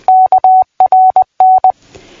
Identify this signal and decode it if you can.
Morse code, heard as a keyed tone, about 20 wpm, 750 Hz. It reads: KRN
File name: PI4AA Uitzending